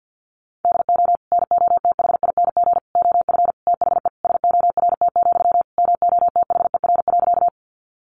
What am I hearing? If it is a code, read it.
Morse code, heard as a keyed tone, about 50 wpm, 700 Hz. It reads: DO NOTHING OF THE SORT, MOTHER.
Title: Sentences from Top 1300 Words 50wpm